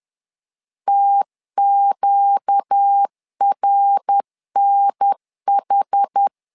Morse code, heard as a keyed tone, about 10 wpm, 800 Hz. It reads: TQRNH
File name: LEZIONI / 4 Q 10wpm 2210